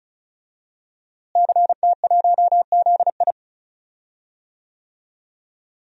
Morse code, heard as a keyed tone, about 36 wpm, 700 Hz. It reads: CT1ZI
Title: DXCC-100 Call Signs - 5 Characters - Repeated 2x - 35wpm